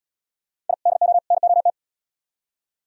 Morse code, heard as a keyed tone, about 70 wpm, 700 Hz. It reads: I DO NOT